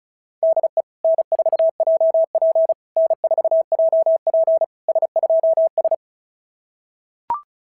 Morse code, heard as a keyed tone, about 33 wpm, 650 Hz. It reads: DE N4JP N4JP S2S
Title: SOTA QSO Practice 35wpm